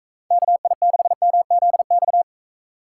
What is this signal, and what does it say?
Morse code, heard as a keyed tone, about 35 wpm, 700 Hz. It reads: KI6MZX